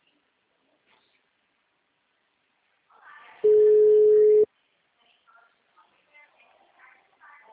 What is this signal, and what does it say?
Ringback tone (Europe, 425 Hz), with speech.